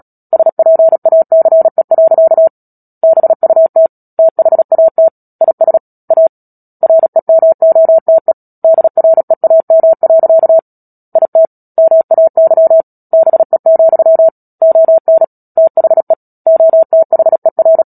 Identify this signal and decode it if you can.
Morse code, heard as a keyed tone, about 37 wpm, 650 Hz. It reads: SPACE. BUT THAT IS A REMOTE DREAM. IT MAY BE, ON THE OTHER
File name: Book Two / 10-Chapter 10 - Epilogue - 36 WPM